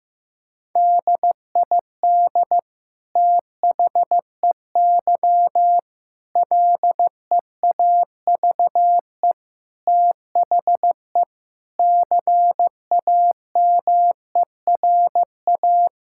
Morse code, heard as a keyed tone, about 16 wpm, 700 Hz. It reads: DID THEY LEAVE THE CAMERA
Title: Sentences from Top 1000 Words - Review Part 5 - 15wpm